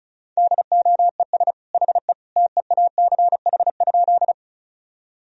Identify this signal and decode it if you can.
Morse code, heard as a keyed tone, about 36 wpm, 700 Hz. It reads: DOES HE TEACH?
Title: Sentences from Top 1000 Words - Review Part 2 - 35wpm